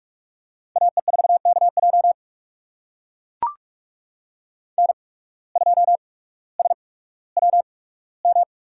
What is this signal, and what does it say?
Morse code, heard as a keyed tone, about 44 wpm, spaced as for 19 wpm, 700 Hz. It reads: AE4KJ N2SWM
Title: USA Call Signs - 5 Characters - 8x Character Spacing - 45wpm